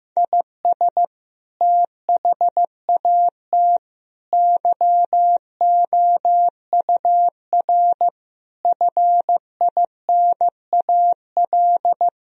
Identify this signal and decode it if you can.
Morse code, heard as a keyed tone, about 15 wpm, 700 Hz. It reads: IS THAT YOUR FINAL
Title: Sentences from Top 1000 Words - Review Part 2 - 15wpm